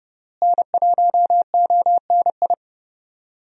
Morse code, heard as a keyed tone, about 30 wpm, 700 Hz. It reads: N1ONI
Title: USA Call Signs - 5 Characters - Repeated 3x - 30wpm